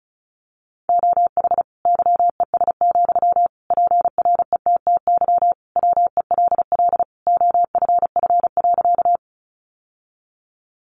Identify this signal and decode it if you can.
Morse code, heard as a keyed tone, about 34 wpm, 700 Hz. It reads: OH YES, PRETTY WELL OFF.
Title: Sentences from Top 1000 Words 35wpm